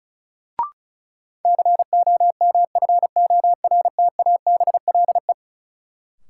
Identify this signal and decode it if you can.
Morse code, heard as a keyed tone, about 36 wpm, 700 Hz. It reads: COMFORTABLE
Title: Sentences from Top 1500 Words - New Word Drill Alt - 35wpm